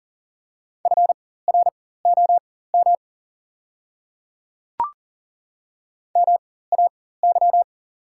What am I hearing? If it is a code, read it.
Morse code, heard as a keyed tone, about 40 wpm, spaced as for 19 wpm, 700 Hz. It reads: FROM MAY
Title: 4x Character Spacing - Top 500 Words - 40wpm